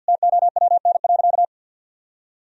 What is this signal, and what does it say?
Morse code, heard as a keyed tone, about 50 wpm, 700 Hz. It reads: TOWN.
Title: Sentences from Top 1000 Words - Review Part 4 - 50wpm